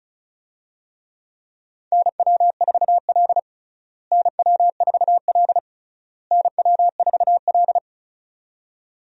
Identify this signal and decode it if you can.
Morse code, heard as a keyed tone, about 36 wpm, 700 Hz. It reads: NW4L NW4L NW4L
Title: USA Call Signs - 4 Characters - Repeated 3x - 35wpm